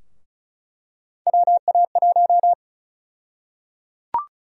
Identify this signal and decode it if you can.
Morse code, heard as a keyed tone, about 34 wpm, 700 Hz. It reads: WA1
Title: USA Call Sign Prefixes - Repeated 2x - 35wpm